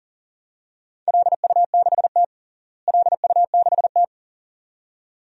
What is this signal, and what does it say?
Morse code, heard as a keyed tone, about 37 wpm, 700 Hz. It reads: LU6T LU6T
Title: DXCC-100 Call Signs - 4 Characters - Repeated 2x - 40wpm